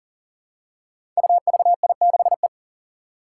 Morse code, heard as a keyed tone, about 38 wpm, 700 Hz. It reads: UVI6E
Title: Code Groups - 5 Letter:Numbers - 40wpm